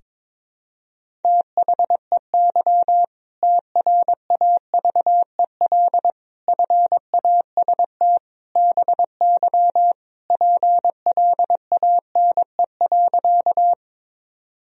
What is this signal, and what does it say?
Morse code, heard as a keyed tone, about 22 wpm, 700 Hz. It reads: THEY TRAVEL FAST BY PLANE.
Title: Sentences from Top 1000 Words - Review Part 3 - 22wpm